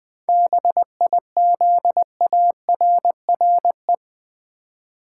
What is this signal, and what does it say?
Morse code, heard as a keyed tone, about 20 wpm, 700 Hz. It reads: BIZARRE
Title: 7 Letter Words - First 3 Letters Given - 20wpm